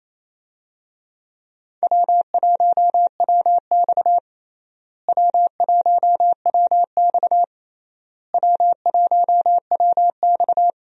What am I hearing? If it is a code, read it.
Morse code, heard as a keyed tone, about 28 wpm, 700 Hz. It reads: W1WX W1WX W1WX
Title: USA Call Signs - 4 Characters - Repeated 3x - 28wpm